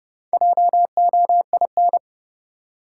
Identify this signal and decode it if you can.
Morse code, heard as a keyed tone, about 30 wpm, 700 Hz. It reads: JOIN
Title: Sentences from Top 1000 Words - Review Part 6 - New Word Drill - 30wpm